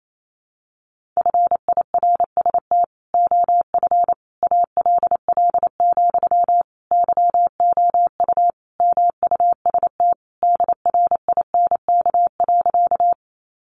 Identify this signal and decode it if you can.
Morse code, heard as a keyed tone, about 28 wpm, 700 Hz. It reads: FIRST OF ALL, YOU MUST DRINK.
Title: Sentences from Top 1500 Words 28wpm